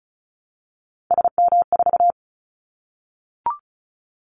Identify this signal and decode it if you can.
Morse code, heard as a keyed tone, about 34 wpm, 700 Hz. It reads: SM4
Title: Callsign Prefixes for Top 25 DXCC - 35wpm